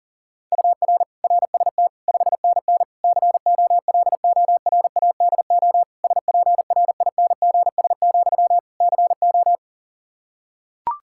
Morse code, heard as a keyed tone, about 40 wpm, 700 Hz. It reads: UR RST 5NN COLORADO SPRINGS, CO
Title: Practice QSO - Exchange 1 - Beginning - 40wpm